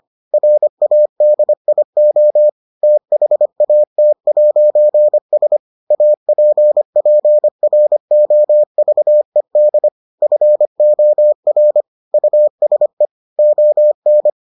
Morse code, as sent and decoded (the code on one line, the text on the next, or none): .-. .- -.. .. --- / - .... .- - .----. ... / .- .--. .--. .-. --- ...- . -.. / ..-. --- .-. / ..- ... / --- -.
RADIO THAT'S APPROVED FOR USE ON